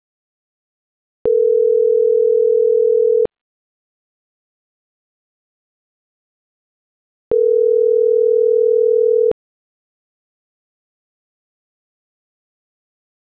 Ringback tone (North America), 440 + 480 Hz.